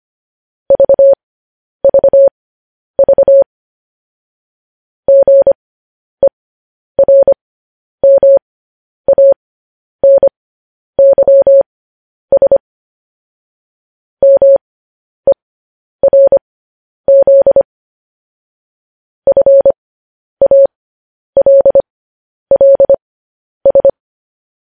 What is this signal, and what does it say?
Morse code, heard as a keyed tone, about 25 wpm, spaced as for 10 wpm, 550 Hz. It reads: VVV GERMANYS MERZ FALLS
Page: news-headlines-in-morse-code-at-10-wpm